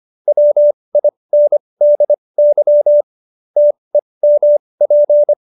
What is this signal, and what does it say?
Morse code, heard as a keyed tone, about 25 wpm, spaced as for 20 wpm, 590 Hz. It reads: WINDY TEMP